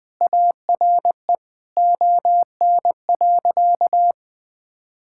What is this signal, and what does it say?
Morse code, heard as a keyed tone, about 19 wpm, 700 Hz. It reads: ARE ON.
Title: Sentences with 2-Word Phrases - 20wpm